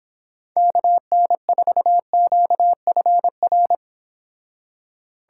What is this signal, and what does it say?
Morse code, heard as a keyed tone, about 26 wpm, 700 Hz. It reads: KN4QFR